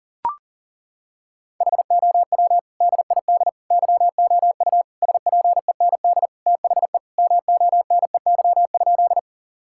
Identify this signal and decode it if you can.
Morse code, heard as a keyed tone, about 37 wpm, 700 Hz. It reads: HOW DID YOU SPEND THE MONEY?
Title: Sentences from Top 1000 Words - Review Part 5 - 40wpm